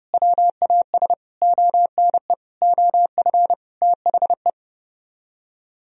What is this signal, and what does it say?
Morse code, heard as a keyed tone, about 40 wpm, 700 Hz. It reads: WAS ONE OF THE